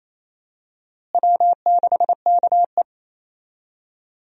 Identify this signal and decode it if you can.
Morse code, heard as a keyed tone, about 28 wpm, 700 Hz. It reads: W6KE
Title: USA Call Signs - 4 Characters - Repeated 3x - 28wpm